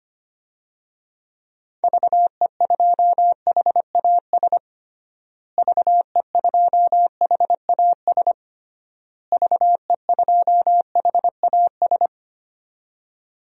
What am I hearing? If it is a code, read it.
Morse code, heard as a keyed tone, about 25 wpm, 700 Hz. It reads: VE2HAS VE2HAS VE2HAS
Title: DXCC-100 Call Signs - 6 Characters - Repeated 3x - 25wpm